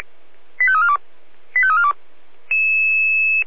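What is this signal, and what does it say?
Five-tone selective call (ZVEI), heard as quick runs of five tones, tones of about 70 ms. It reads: 85342 85342
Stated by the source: ZVEI five-tone selective call 85340 received by SDR.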